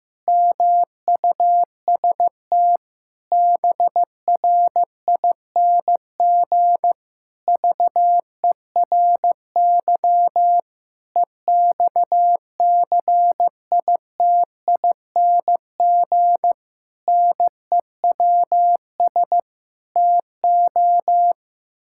Morse code, heard as a keyed tone, about 16 wpm, 700 Hz. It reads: MUST BRING VERY EXCITING NEWS TO
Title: Pangram Challenge v2 - 15wpm